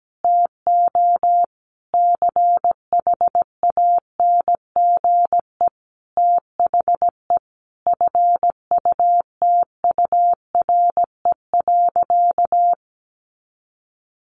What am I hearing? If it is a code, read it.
Morse code, heard as a keyed tone, about 17 wpm, 700 Hz. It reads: TO CHANGE THE FUTURE.
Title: A Sound Of Thunder 17wpm